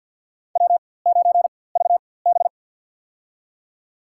Morse code, heard as a keyed tone, about 47 wpm, spaced as for 23 wpm, 700 Hz. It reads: W9VB
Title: USA Call Signs - 4x Character Spacing - 50wpm